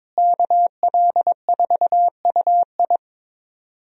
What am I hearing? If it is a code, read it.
Morse code, heard as a keyed tone, about 22 wpm, 700 Hz. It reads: KL4UI
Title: USA Call Signs - 5 Characters - Repeated 3x - 22wpm